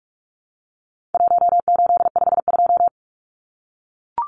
Morse code, heard as a keyed tone, about 44 wpm, 700 Hz. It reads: 1852
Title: Numbers - 4-Digit 45wpm